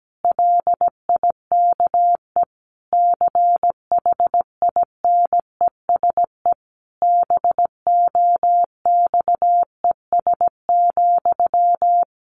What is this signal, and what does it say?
Morse code, heard as a keyed tone, about 17 wpm, 700 Hz. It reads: LIKE CHINESE BOXES,
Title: A Sound Of Thunder 17wpm